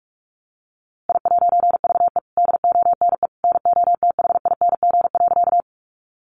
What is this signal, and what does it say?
Morse code, heard as a keyed tone, about 44 wpm, 700 Hz. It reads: I'VE DONE NOTHING.
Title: A Sound Of Thunder 45wpm